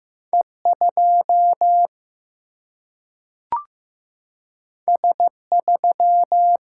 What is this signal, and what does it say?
Morse code, heard as a keyed tone, about 15 wpm, 700 Hz. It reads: E2 S3
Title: Single-Letter Single-Number 15wpm